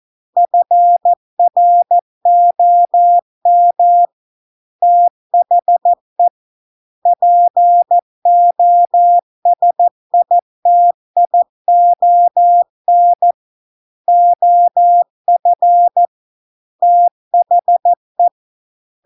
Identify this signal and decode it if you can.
Morse code, heard as a keyed tone, about 14 wpm, 700 Hz. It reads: FROM THE POSITION OF THE